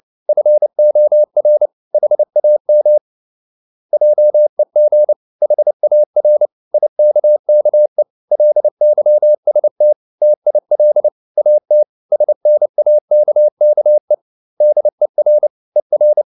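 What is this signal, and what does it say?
Morse code, heard as a keyed tone, about 29 wpm, 600 Hz. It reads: FOR HAM JEG HAR IKKE LYST TIL AT SNAKKE DER ER
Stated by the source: Fast (QRQ) Morse at 29 wpm, from CW med Gnister.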